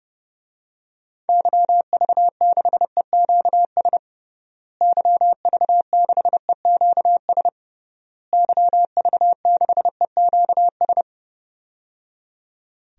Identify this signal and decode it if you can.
Morse code, heard as a keyed tone, about 30 wpm, 700 Hz. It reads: YV6EQS YV6EQS YV6EQS
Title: DXCC-100 Call Signs - 6 Characters - Repeated 3x - 30wpm